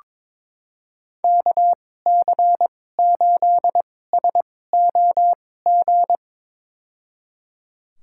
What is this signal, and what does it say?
Morse code, heard as a keyed tone, about 22 wpm, spaced as for 16 wpm, 700 Hz. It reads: KC8SOG
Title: USA Call Signs - 2x Character Spacing - 22wpm